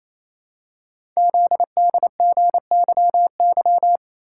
Morse code, heard as a keyed tone, about 28 wpm, 700 Hz. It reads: ZDGYY
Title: Code Groups - 5 Letters - 28wpm